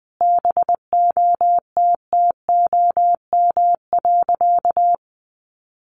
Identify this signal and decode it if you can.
Morse code, heard as a keyed tone, about 19 wpm, 700 Hz. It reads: BOTTOM.
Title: Easy Phrases 20wpm